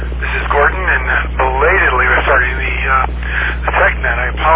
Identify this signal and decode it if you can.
Emergency Alert System (SAME) tones, heard as harsh buzzing bursts, with speech. It is heard from 0.7 to 4.3 s.